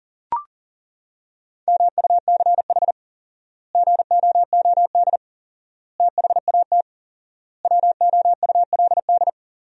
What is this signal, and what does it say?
Morse code, heard as a keyed tone, about 40 wpm, 700 Hz. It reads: MUCH GOOD THAT WOULD
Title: 4x Word-Spacing - Sentences from Top 200 Words - 40wpm